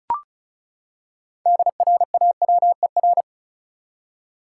Morse code, heard as a keyed tone, about 36 wpm, 700 Hz. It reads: DRAWER
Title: Six Letter Words - 35wpm